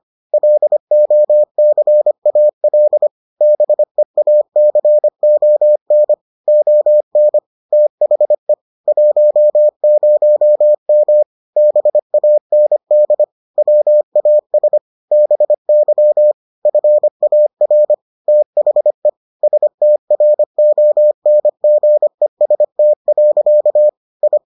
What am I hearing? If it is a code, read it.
Morse code, heard as a keyed tone, about 25 wpm, 600 Hz. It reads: LOCAL BEACON ON THE 10M BAND WAS BY FAR THE STRONGEST. I